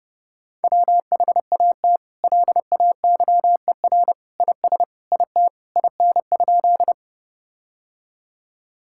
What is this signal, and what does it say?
Morse code, heard as a keyed tone, about 32 wpm, 700 Hz. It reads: WHAT LAYER IS IT IN?
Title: Sentences from Top 1600 Words 30wpm